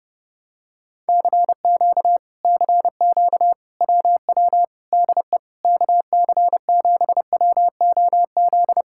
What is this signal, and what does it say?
Morse code, heard as a keyed tone, about 29 wpm, 700 Hz. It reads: CQ CQ WW DE KC7WOZ